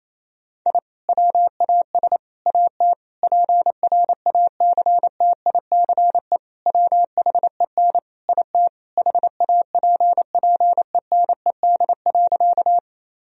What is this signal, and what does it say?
Morse code, heard as a keyed tone, about 28 wpm, 700 Hz. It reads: I WAS AT PRACTICE WHEN IT HAPPENED.
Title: Sentences from Top 1000 Words - Review Part 4 - 28wpm